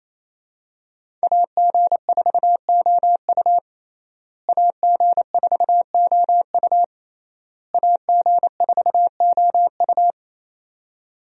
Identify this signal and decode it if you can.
Morse code, heard as a keyed tone, about 28 wpm, 700 Hz. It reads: AG4OU AG4OU AG4OU
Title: USA Call Signs - 5 Characters - Repeated 3x - 28wpm